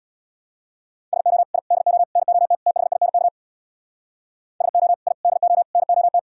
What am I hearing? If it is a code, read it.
Morse code, heard as a keyed tone, about 59 wpm, 690 Hz. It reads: SO I DO NOT THINK SO I DO NOT
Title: Speed-Racing - Sets of 5 Words 50wpm